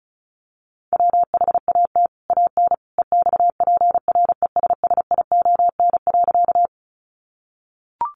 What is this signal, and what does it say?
Morse code, heard as a keyed tone, about 34 wpm, 700 Hz. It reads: WHAT AN EXPRESSION.
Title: Sentences from Top 1200 Words 35wpm